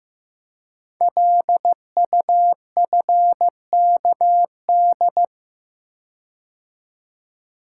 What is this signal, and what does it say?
Morse code, heard as a keyed tone, about 15 wpm, 700 Hz. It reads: LUFKD
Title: Code Groups - 5 Letters - 15wpm